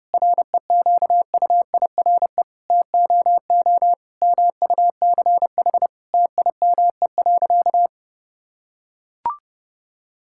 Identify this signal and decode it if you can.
Morse code, heard as a keyed tone, about 30 wpm, 700 Hz. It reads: REQUIRE TOO MUCH TIME.
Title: Sentences from Top 1000 Words - Review Part 6 - 30wpm